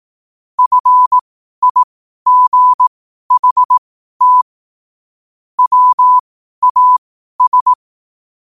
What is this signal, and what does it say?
Morse code, heard as a keyed tone, about 18 wpm, spaced as for 12 wpm, 1000 Hz. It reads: FIGHT WAS
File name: qotdfiles / 12wpm